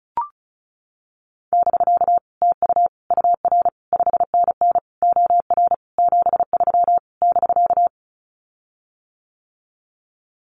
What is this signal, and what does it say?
Morse code, heard as a keyed tone, about 36 wpm, 700 Hz. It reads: <BK> TU UR 5NN OR 73 <BK>
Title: Common QSO Phrases Encore 35wpm